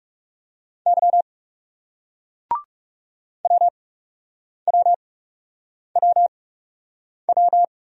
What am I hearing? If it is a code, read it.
Morse code, heard as a keyed tone, about 40 wpm, spaced as for 8 wpm, 700 Hz. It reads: Y WWWW